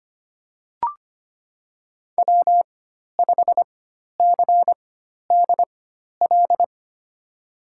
Morse code, heard as a keyed tone, about 25 wpm, spaced as for 12 wpm, 700 Hz. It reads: W5CDL
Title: USA Call Signs - 5 Characters - 4x Character Spacing - 25wpm